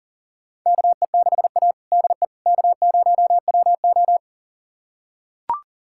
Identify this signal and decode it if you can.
Morse code, heard as a keyed tone, about 37 wpm, 700 Hz. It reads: KE6A DE K0WO
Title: USA Callsign DE Callsign - 4 Characters - 40wpm